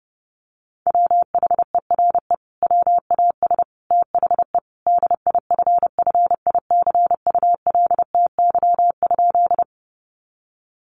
Morse code, heard as a keyed tone, about 30 wpm, 700 Hz. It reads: WHERE WAS THE DIFFICULTY?
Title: Sentences from Top 1500 Words 30wpm